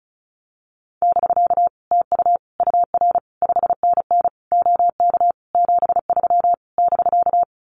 Morse code, heard as a keyed tone, about 34 wpm, 700 Hz. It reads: <BK> TU UR 5NN OK 73 <BK>
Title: Common QSO Phrases Encore 35wpm